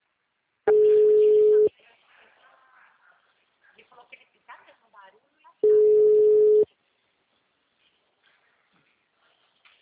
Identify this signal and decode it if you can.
Ringback tone (Europe, 425 Hz), with speech.